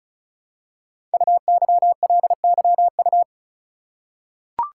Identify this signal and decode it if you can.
Morse code, heard as a keyed tone, about 36 wpm, 700 Hz. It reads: UYLYU
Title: Code Groups - 5 Letters - 35wpm